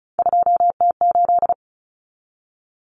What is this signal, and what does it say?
Morse code, heard as a keyed tone, about 34 wpm, 700 Hz. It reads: 2T8
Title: Common Cut Numbers - 3-Digit 35wpm